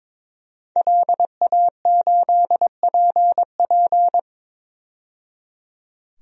Morse code, heard as a keyed tone, about 22 wpm, 700 Hz. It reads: LA8PP
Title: DXCC-100 Call Signs - 5 Characters - 22wpm